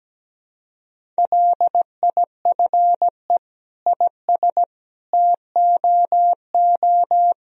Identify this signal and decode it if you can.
Morse code, heard as a keyed tone, about 17 wpm, 700 Hz. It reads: LIFE IS TOO
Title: Sentences from Top 1000 Words - Review Part 2 - 17wpm